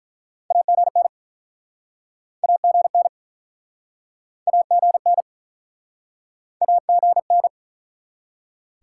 Morse code, heard as a keyed tone, about 48 wpm, 700 Hz. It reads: AGN AGN AGN AGN